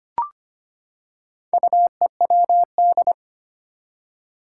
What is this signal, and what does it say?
Morse code, heard as a keyed tone, about 25 wpm, 700 Hz. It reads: UEWD
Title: Four Challenging Letter-Number Combinations 25wpm